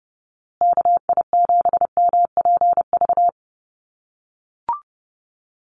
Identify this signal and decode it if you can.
Morse code, heard as a keyed tone, about 30 wpm, 700 Hz. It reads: KI7MPV
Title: USA Call Signs Repeated 3x 30wpm